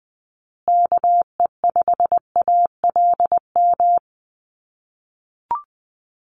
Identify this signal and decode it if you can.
Morse code, heard as a keyed tone, about 20 wpm, 700 Hz. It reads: KE5ALM
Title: USA Call Signs Repeated 3x 20wpm